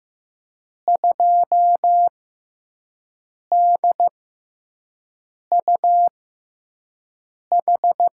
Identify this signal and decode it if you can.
Morse code, heard as a keyed tone, about 15 wpm, spaced as for 5 wpm, 700 Hz. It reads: 2DUH
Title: USA Call Signs - 5 Characters - 6x Character Spacing - 15wpm